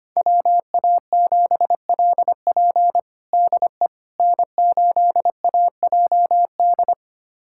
Morse code, heard as a keyed tone, about 25 wpm, 700 Hz. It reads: WA7LP DE N8AJD
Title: USA Callsign DE Callsign - 5 Characters - 25wpm